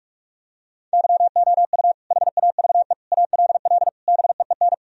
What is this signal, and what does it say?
Morse code, heard as a keyed tone, about 46 wpm, 700 Hz. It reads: YOU HAVE ALL BEEN
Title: Sentences from Top 1000 Words - Review Part 2 - 45wpm